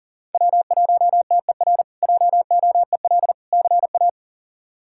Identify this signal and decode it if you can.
Morse code, heard as a keyed tone, about 40 wpm, 700 Hz. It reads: W1TER JOEL CA
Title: K1USN SST QSO Practice - 40wpm